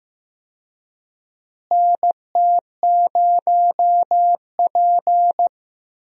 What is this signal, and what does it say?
Morse code, heard as a keyed tone, about 15 wpm, 700 Hz. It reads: NT0P
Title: USA Call Signs - 4 Characters - Repeated 3x - 15wpm